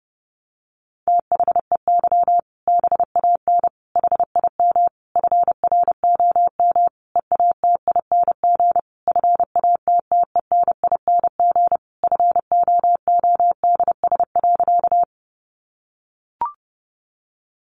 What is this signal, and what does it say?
Morse code, heard as a keyed tone, about 30 wpm, 700 Hz. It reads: THEY BAN HIM FROM EATING FATTENING FOODS.
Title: Easy Phrases 30wpm